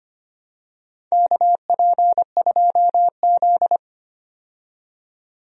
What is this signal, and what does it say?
Morse code, heard as a keyed tone, about 25 wpm, 700 Hz. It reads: KP2Z
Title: DXCC-100 Call Signs - 4 Characters - 25wpm